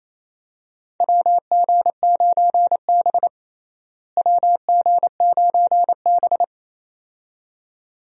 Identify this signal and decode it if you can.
Morse code, heard as a keyed tone, about 28 wpm, 700 Hz. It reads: WG9B WG9B
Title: USA Call Signs - 4 Characters - Repeated 2x - 28wpm